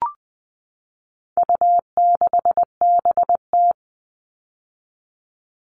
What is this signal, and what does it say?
Morse code, heard as a keyed tone, about 20 wpm, 700 Hz. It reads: U6BT
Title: All Cut Numbers - 4-Digit 20wpm